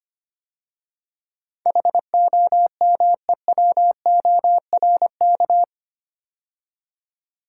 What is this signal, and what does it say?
Morse code, heard as a keyed tone, about 25 wpm, 700 Hz. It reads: HOMEWORK
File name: 8 Letter Words - First 3 Letters Given - 25wpm